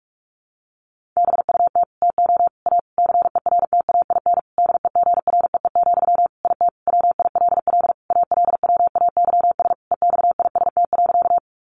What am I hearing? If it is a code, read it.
Morse code, heard as a keyed tone, about 44 wpm, 700 Hz. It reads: BUT TO A CERTAIN DEGREE, IT WILL ALWAYS EXIST.
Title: Sentences from Top 1500 Words 45wpm